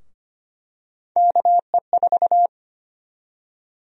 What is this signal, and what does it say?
Morse code, heard as a keyed tone, about 25 wpm, 700 Hz. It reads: KE4
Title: USA Call Sign Prefixes - Repeated 2x - 25wpm